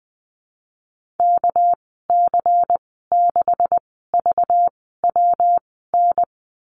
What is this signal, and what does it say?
Morse code, heard as a keyed tone, about 19 wpm, spaced as for 14 wpm, 700 Hz. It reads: KC6VWN
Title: 213 Lesson 34g - Number 8 - Call Signs 20wpm